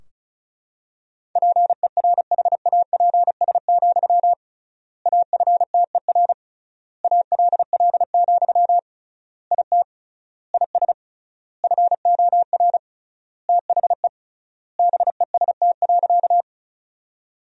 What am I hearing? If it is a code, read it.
Morse code, heard as a keyed tone, about 34 wpm, 700 Hz. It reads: PERHAPS, AFTER ALL, IT IS FOR THE BEST.
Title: 3x Word-Spacing - Sentences from Top 500 Words - 35wpm